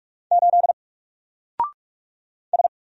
Morse code, heard as a keyed tone, about 45 wpm, 700 Hz. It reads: 8 S
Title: Single Letter-Number 45wpm